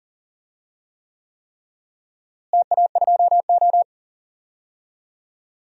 Morse code, heard as a keyed tone, about 40 wpm, 700 Hz. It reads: TA2O
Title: DXCC-100 Call Signs - 4 Characters - Repeated 3x - 40wpm